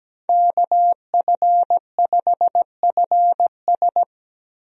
Morse code, heard as a keyed tone, about 17 wpm, 700 Hz. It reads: KF5FS
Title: USA Call Signs - 5 Characters - 17wpm